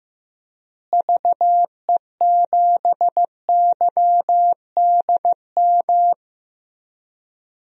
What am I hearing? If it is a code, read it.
Morse code, heard as a keyed tone, about 15 wpm, 700 Hz. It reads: VE7YDM
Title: DXCC-100 Call Signs - 6 Characters - Repeated 2x - 15wpm